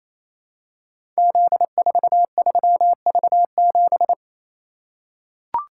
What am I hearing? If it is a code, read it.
Morse code, heard as a keyed tone, about 29 wpm, 700 Hz. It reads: Z43V7
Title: Code Groups - 5 Letter:Numbers - 28wpm